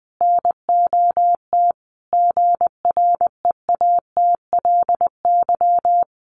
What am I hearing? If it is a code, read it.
Morse code, heard as a keyed tone, about 19 wpm, 700 Hz. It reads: NOT GREATLY
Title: Sentences from Top 1400 Words 20wpm